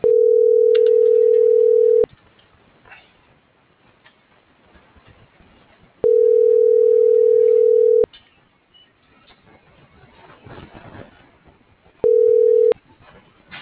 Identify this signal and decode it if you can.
Ringback tone (North America), 440 + 480 Hz.
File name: CENTRAL MABE / out xquenda.trinidad